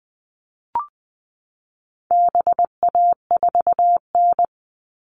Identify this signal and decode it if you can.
Morse code, heard as a keyed tone, about 20 wpm, 700 Hz. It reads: BA4N
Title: All Cut Numbers - 4-Digit 20wpm